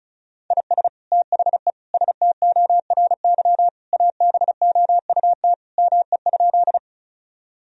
Morse code, heard as a keyed tone, about 34 wpm, 700 Hz. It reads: IS THE STORY ABOUT ME?
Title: Sentences from Top 1000 Words - Review Part 1 - 35wpm